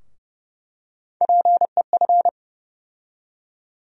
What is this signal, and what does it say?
Morse code, heard as a keyed tone, about 30 wpm, 700 Hz. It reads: PEF
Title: USA Call Sign Postfixes - 3 Letters 30wpm